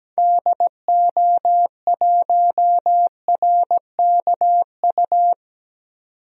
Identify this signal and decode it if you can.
Morse code, heard as a keyed tone, about 17 wpm, 700 Hz. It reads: DO1RKU
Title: DXCC-100 Call Signs - 6 Characters - 17wpm